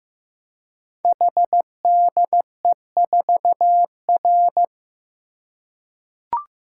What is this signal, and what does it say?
Morse code, heard as a keyed tone, about 15 wpm, 700 Hz. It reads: HDE4R
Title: Code Groups - 5 Letter:Numbers - 15wpm